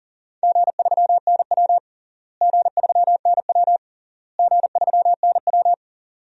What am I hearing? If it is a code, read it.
Morse code, heard as a keyed tone, about 40 wpm, 700 Hz. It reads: G3NW G3NW G3NW
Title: DXCC-100 Call Signs - 4 Characters - Repeated 3x - 40wpm